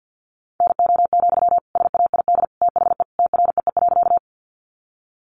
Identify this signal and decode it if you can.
Morse code, heard as a keyed tone, about 50 wpm, 700 Hz. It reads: NO, SAID THE TREE.
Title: Sentences from Top 300 Words 50wpm